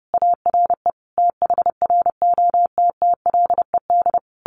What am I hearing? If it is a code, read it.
Morse code, heard as a keyed tone, about 30 wpm, 700 Hz. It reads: ARE THROTTLED
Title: A Sound Of Thunder 30wpm